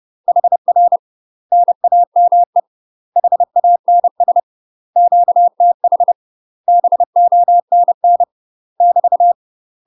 Morse code, heard as a keyed tone, about 30 wpm, 700 Hz. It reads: HR NAME HANS QTH BONN <BT>
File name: UK MASTER QSOs TRAINING / 30WPM / UK MASTER QSOs 0044